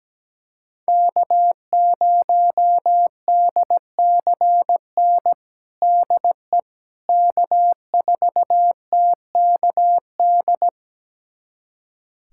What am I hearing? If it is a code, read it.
Morse code, heard as a keyed tone, about 17 wpm, 700 Hz. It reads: K0DCN DE K4TKD
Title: USA Callsign DE Callsign - 5 Characters - 17wpm